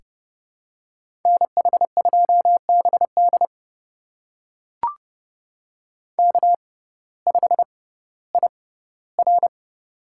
Morse code, heard as a keyed tone, about 30 wpm, spaced as for 16 wpm, 700 Hz. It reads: NH2BD K5IR
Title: USA Call Signs - 6x Character Spacing - 30wpm